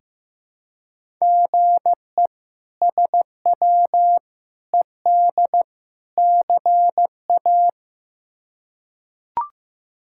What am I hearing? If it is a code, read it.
Morse code, heard as a keyed tone, about 15 wpm, 700 Hz. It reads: GE SW ED CA
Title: K1USN SST QSO Practice - 15wpm